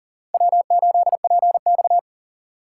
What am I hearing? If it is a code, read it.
Morse code, heard as a keyed tone, about 38 wpm, 700 Hz. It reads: W8PX
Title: USA Call Signs - 4 Characters - 4x Character Spacing - 40wpm